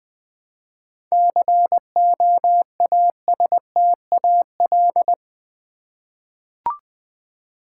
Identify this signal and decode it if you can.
Morse code, heard as a keyed tone, about 20 wpm, 690 Hz. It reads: COASTAL
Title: 7 Letter Words - First 3 Letters Given - 20wpm